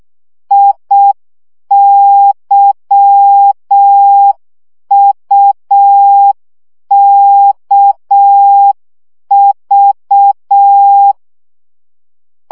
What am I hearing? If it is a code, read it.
Morse code, heard as a keyed tone, about 6 wpm, 800 Hz. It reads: IYUKV
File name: morze vizsga 2011-1 5